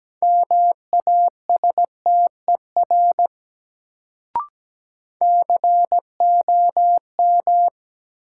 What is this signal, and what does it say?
Morse code, heard as a keyed tone, about 17 wpm, 700 Hz. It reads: MASTER COM
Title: Sentences from Top 1300 Words - New Word Drill Alt - 17wpm